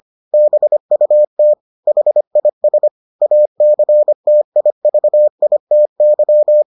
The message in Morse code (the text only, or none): BUT HIS ACTIVITY